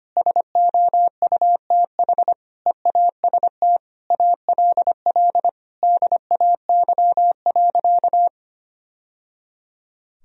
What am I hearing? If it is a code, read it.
Morse code, heard as a keyed tone, about 25 wpm, 700 Hz. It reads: SOUTH EAST ALL DAY.
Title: Sentences from Top 1000 Words - Review Part 3 - 25wpm